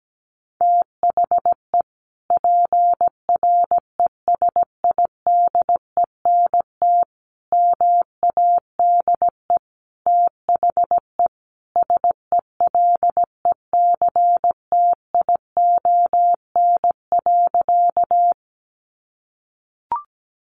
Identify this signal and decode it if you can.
Morse code, heard as a keyed tone, about 17 wpm, 700 Hz. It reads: THE PRESIDENT MADE THE SELECTION.
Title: Sentences from Top 1400 Words 17wpm